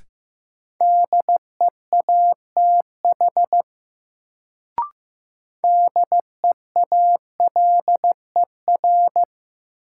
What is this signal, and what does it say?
Morse code, heard as a keyed tone, about 15 wpm, 700 Hz. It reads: DEATH DEALER
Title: Sentences from Top 1100 Words - New Word Drill Alt - 15wpm